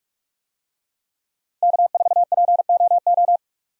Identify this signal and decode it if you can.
Morse code, heard as a keyed tone, about 44 wpm, 700 Hz. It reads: K4POO
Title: USA Call Signs - 5 Characters - 45wpm